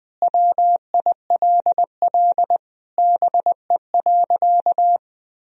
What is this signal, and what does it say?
Morse code, heard as a keyed tone, about 20 wpm, 700 Hz. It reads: WILL BE.
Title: Sentences from Top 1000 Words - Review Part 4 - 20wpm